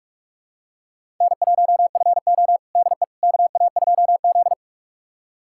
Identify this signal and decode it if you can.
Morse code, heard as a keyed tone, about 45 wpm, 700 Hz. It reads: N1FO DE KA2Z
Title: USA Callsign DE Callsign - 4 Characters - 45wpm